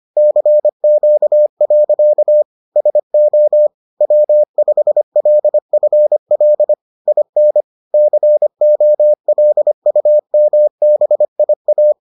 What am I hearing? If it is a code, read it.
Morse code, heard as a keyed tone, about 25 wpm, 600 Hz. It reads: CQ. SO W5LFL IN COLUMBIA